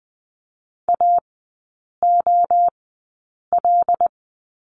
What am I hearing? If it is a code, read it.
Morse code, heard as a keyed tone, about 20 wpm, spaced as for 8 wpm, 700 Hz. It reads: AOL
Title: 093 Lesson 16e - Letter c - Three Characters 20wpm